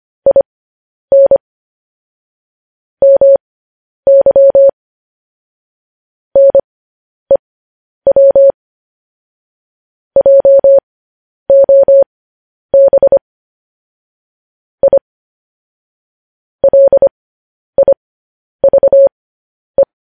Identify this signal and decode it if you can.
Morse code, heard as a keyed tone, about 25 wpm, spaced as for 10 wpm, 550 Hz. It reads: IN MY NEW JOB I LIVE
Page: news-headlines-in-morse-code-at-10-wpm